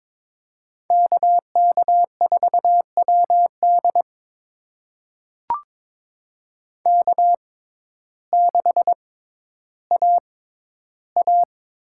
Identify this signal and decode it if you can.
Morse code, heard as a keyed tone, about 21 wpm, spaced as for 12 wpm, 700 Hz. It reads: KK4WD K6AA
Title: USA Call Signs - 5 Characters - 6x Character Spacing - 22wpm